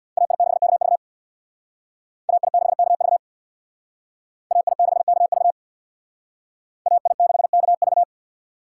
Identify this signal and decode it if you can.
Morse code, heard as a keyed tone, about 62 wpm, 700 Hz. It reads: AI6KV AI6KV AI6KV AI6KV
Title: Speed Racing - USA 2x2 Call Signs 50wpm